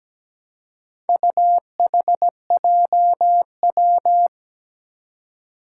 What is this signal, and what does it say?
Morse code, heard as a keyed tone, about 17 wpm, 700 Hz. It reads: UHJW
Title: Four Challenging Letter-Number Combinations 17wpm